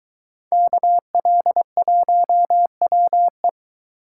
Morse code, heard as a keyed tone, about 21 wpm, 700 Hz. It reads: KL1WE